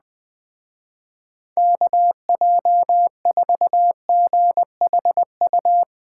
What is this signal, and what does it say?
Morse code, heard as a keyed tone, about 21 wpm, 700 Hz. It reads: KJ4GHU